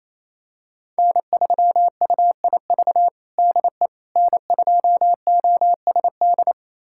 Morse code, heard as a keyed tone, about 27 wpm, 700 Hz. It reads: N3UIV DE N2OSD
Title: USA Callsign DE Callsign - 5 Characters - 28wpm